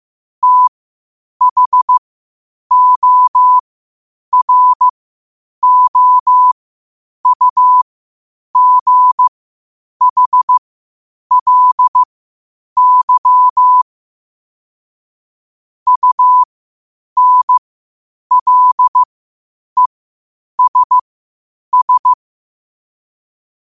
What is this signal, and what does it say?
Morse code, heard as a keyed tone, about 15 wpm, spaced as for 8 wpm, 1000 Hz. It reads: THOROUGHLY UNLESS
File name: qotdfiles / 8wpm